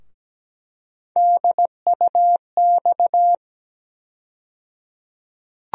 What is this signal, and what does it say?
Morse code, heard as a keyed tone, about 17 wpm, 700 Hz. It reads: DUX
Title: USA Call Sign Postfixes - 3 Letters 17wpm